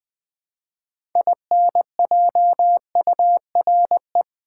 Morse code, heard as a keyed tone, about 20 wpm, 700 Hz. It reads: INJURE
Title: Build A Word - 3 Letters At A Time - 6-Letter Words - 20wpm